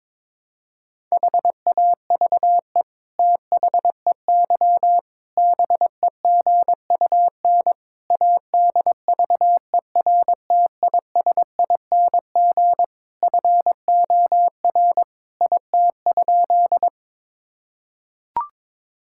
Morse code, heard as a keyed tone, about 22 wpm, 700 Hz. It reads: HAVE THEY BEGUN ADVERTISING FOR IT?
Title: Sentences from Top 1000 Words - Review Part 6 - 22wpm